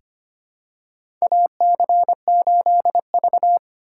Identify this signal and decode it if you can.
Morse code, heard as a keyed tone, about 25 wpm, 700 Hz. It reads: AC8V
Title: USA Call Signs - 4 Characters - 2x Character Spacing - 25wpm